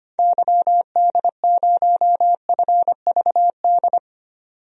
Morse code, heard as a keyed tone, about 25 wpm, 700 Hz. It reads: YD0FVD
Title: DXCC-100 Call Signs - 6 Characters - 25wpm